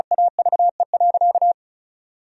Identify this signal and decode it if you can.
Morse code, heard as a keyed tone, about 36 wpm, 700 Hz. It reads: AVE.
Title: Sentences from Top 1000 Words - Review Part 4 - 35wpm